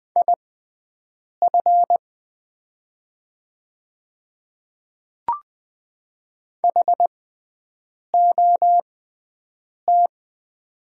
Morse code, heard as a keyed tone, about 20 wpm, spaced as for 7 wpm, 700 Hz. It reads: IF HOT
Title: 6x Character Spacing - Top 300 Words - 20wpm